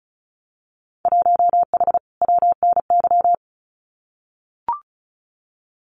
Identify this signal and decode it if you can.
Morse code, heard as a keyed tone, about 36 wpm, 700 Hz. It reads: 1H WNY
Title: Winter Field Day Exchanges 35wpm